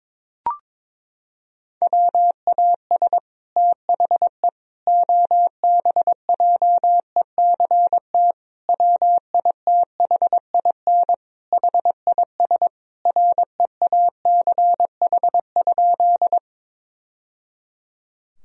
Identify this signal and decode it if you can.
Morse code, heard as a keyed tone, about 22 wpm, 700 Hz. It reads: WAS THE OBJECT WITHIN HIS REACH?
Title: Sentences from Top 1000 Words - Review Part 4 - 22wpm